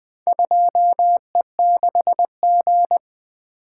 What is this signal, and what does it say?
Morse code, heard as a keyed tone, about 20 wpm, 700 Hz. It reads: 2E6G
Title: Four Letter-Number Combinations - 20wpm